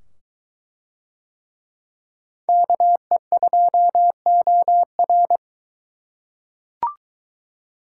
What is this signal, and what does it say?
Morse code, heard as a keyed tone, about 22 wpm, 700 Hz. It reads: KE2OR